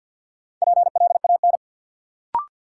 Morse code, heard as a keyed tone, about 48 wpm, 700 Hz. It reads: PLAN
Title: Sentences from Top 500 Words - New Word Drill - 50wpm